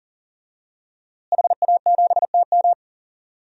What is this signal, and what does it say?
Morse code, heard as a keyed tone, about 38 wpm, 700 Hz. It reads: HA7TM
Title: DXCC-100 Call Signs - 5 Characters - Repeated 2x - 40wpm